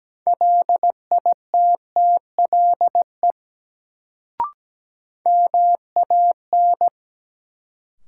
Morse code, heard as a keyed tone, about 17 wpm, 700 Hz. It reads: LITTLE MAN
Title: Sentences from Top 200 Words - New Word Drill - 17wpm